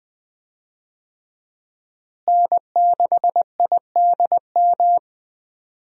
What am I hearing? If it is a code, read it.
Morse code, heard as a keyed tone, about 20 wpm, 700 Hz. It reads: N6IDM
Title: USA Call Signs - 5 Characters - Repeated 3x - 20wpm